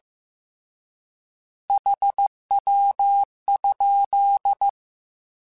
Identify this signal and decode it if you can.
Morse code, heard as a keyed tone, about 15 wpm, 800 Hz. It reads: HW?
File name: Abreviations / Abreviations mp3s / Abreviations@13